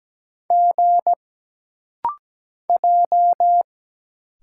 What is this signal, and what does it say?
Morse code, heard as a keyed tone, about 17 wpm, 700 Hz. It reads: G J
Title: Single Letters - Rapid-Fire 17wpm